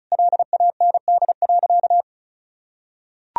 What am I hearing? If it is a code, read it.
Morse code, heard as a keyed tone, about 34 wpm, 700 Hz. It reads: LAND.
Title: Sentences from Top 1000 Words - Review Part 2 - 35wpm